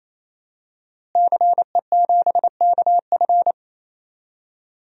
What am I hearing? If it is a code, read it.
Morse code, heard as a keyed tone, about 28 wpm, 700 Hz. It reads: CE7KF
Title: DXCC-100 Call Signs - 5 Characters - Repeated 3x - 28wpm